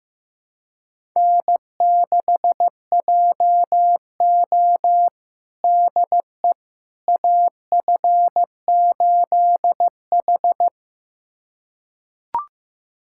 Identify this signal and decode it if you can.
Morse code, heard as a keyed tone, about 15 wpm, 700 Hz. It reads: N6JO DE AF8H
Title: USA Callsign DE Callsign - 4 Characters - 15wpm